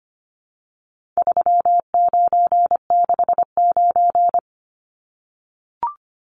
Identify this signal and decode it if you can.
Morse code, heard as a keyed tone, about 25 wpm, 700 Hz. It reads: 3969
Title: Numbers - 4-Digit 25wpm